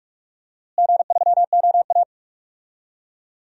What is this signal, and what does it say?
Morse code, heard as a keyed tone, about 41 wpm, 700 Hz. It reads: G3OA
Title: DXCC-100 Call Signs - 4 Characters - Repeated 3x - 45wpm